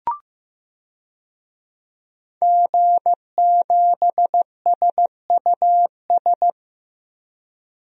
Morse code, heard as a keyed tone, about 16 wpm, 700 Hz. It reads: G7SUS
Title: DXCC-100 Call Signs - 5 Characters - Repeated 3x - 15wpm